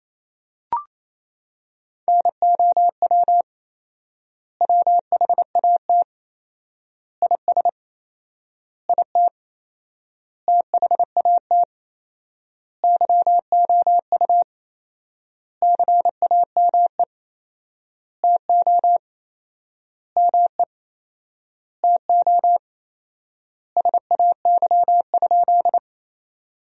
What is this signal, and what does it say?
Morse code, heard as a keyed tone, about 28 wpm, 700 Hz. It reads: NOW WHAT IS IT THAT YOU CAME TO ME TO SAY?
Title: 4x Word-Spacing - Sentences from Top 200 Words - 28wpm